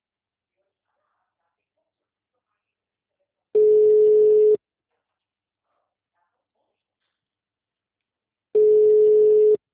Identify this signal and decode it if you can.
Ringback tone (Europe, 425 Hz).
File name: out- -2019- - - .135